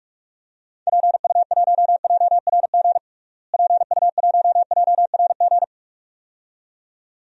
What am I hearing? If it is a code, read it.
Morse code, heard as a keyed tone, about 45 wpm, 700 Hz. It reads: PU1JRG PU1JRG
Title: DXCC-100 Call Signs - 6 Characters - Repeated 2x - 45wpm